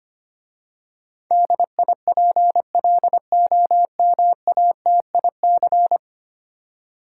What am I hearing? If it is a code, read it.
Morse code, heard as a keyed tone, about 25 wpm, 700 Hz. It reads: DIPLOMATIC
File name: 10 Letter Words - First 3 Letters Given - 25wpm